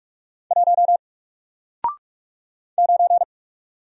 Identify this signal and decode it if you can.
Morse code, heard as a keyed tone, about 46 wpm, 700 Hz. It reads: 1 9
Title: Single Character (Letter:Number:Punctuation:Prosign) Alt 45wpm